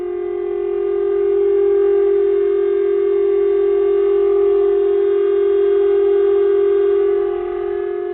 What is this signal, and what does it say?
Telephone call progress tone (Japan), heard as a telephone network's tone, about 400 Hz. Dial tone.